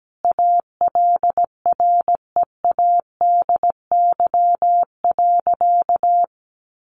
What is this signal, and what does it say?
Morse code, heard as a keyed tone, about 17 wpm, 700 Hz. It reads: ALREADY.
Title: Sentences from Top 1000 Words 17wpm